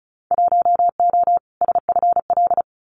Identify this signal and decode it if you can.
Morse code, heard as a keyed tone, about 34 wpm, 700 Hz. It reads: 1O SFL
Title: Winter Field Day Exchanges Encore 35wpm